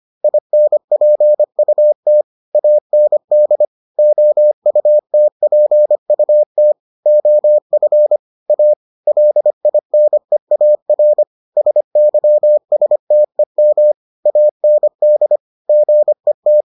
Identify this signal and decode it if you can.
Morse code, heard as a keyed tone, about 25 wpm, 600 Hz. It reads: INPUT AND OUTPUT OF A LINEAR SYSTEM AND GET